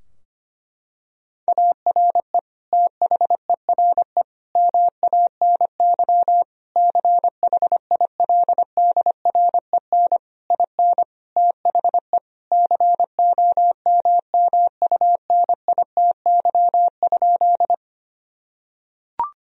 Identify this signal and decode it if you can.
Morse code, heard as a keyed tone, about 25 wpm, 700 Hz. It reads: ARE THERE MANY CHILDREN IN THE COMMUNITY?
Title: Sentences from Top 1000 Words - Review Part 4 - 25wpm